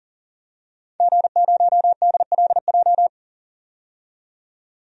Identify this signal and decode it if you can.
Morse code, heard as a keyed tone, about 40 wpm, 700 Hz. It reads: G0DLJ
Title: DXCC-100 Call Signs - 5 Characters - 40wpm